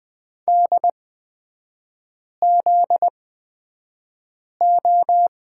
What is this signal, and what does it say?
Morse code, heard as a keyed tone, about 20 wpm, spaced as for 5 wpm, 700 Hz. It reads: DZO
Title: 250 Lesson 40e - BK - Three Characters 20wpm